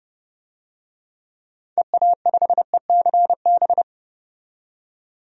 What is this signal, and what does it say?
Morse code, heard as a keyed tone, about 30 wpm, 700 Hz. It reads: EA5ECB
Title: DXCC-100 Call Signs - 6 Characters - 30wpm